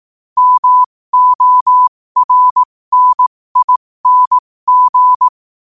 Morse code, heard as a keyed tone, about 18 wpm, spaced as for 15 wpm, 1000 Hz. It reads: MORNING
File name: qotdfiles / 15wpm